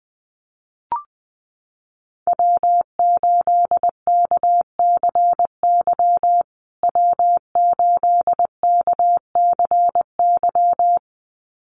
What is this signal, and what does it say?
Morse code, heard as a keyed tone, about 20 wpm, 700 Hz. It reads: W8KCY W8KCY
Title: USA Call Signs Repeated 2x 20wpm